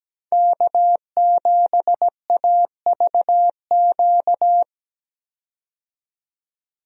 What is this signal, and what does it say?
Morse code, heard as a keyed tone, about 17 wpm, 700 Hz. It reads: K7AVQ
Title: USA Call Signs - 5 Characters - 17wpm